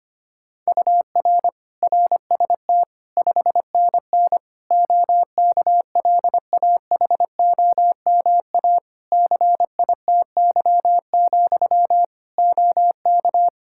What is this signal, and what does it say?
Morse code, heard as a keyed tone, about 25 wpm, 700 Hz. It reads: UR RST 5NN OKLAHOMA CITY, OK
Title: Practice QSO - Exchange 1 - Beginning - 25wpm